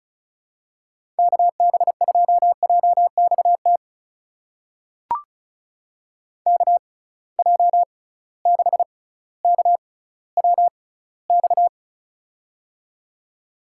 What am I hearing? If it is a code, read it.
Morse code, heard as a keyed tone, about 36 wpm, spaced as for 18 wpm, 700 Hz. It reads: KB2JXT KJ6KWX
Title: USA Call Signs - 6 Characters - 6x Character Spacing - 35wpm